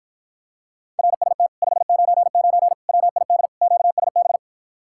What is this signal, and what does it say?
Morse code, heard as a keyed tone, about 54 wpm, 700 Hz. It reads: RST 599 WID QSB